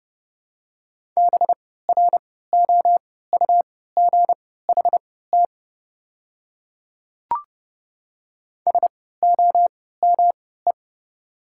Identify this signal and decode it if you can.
Morse code, heard as a keyed tone, about 29 wpm, spaced as for 17 wpm, 700 Hz. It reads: BROUGHT SOME
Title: 3x Character Spacing - Top 500 Words - 30wpm